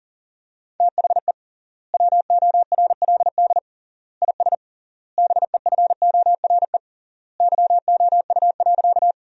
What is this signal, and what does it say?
Morse code, heard as a keyed tone, about 38 wpm, 700 Hz. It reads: THE WORLD IS BEFORE YOU.
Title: 3x Word-Spacing - Sentences from Top 200 Words - 40wpm